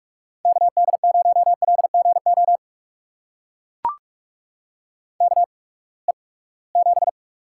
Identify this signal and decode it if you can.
Morse code, heard as a keyed tone, about 44 wpm, spaced as for 25 wpm, 700 Hz. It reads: KD0LGO KE7
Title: USA Call Signs - 6 Characters - 8x Character Spacing - 45wpm